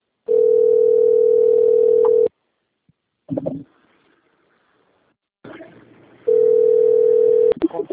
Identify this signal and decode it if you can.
Ringback tone (North America), 440 + 480 Hz.